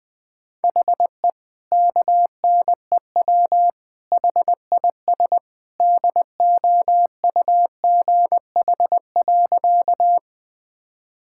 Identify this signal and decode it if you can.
Morse code, heard as a keyed tone, about 19 wpm, 700 Hz. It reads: HE KNEW HIS DOUGH.
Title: All-Age Appropriate Jokes - 20wpm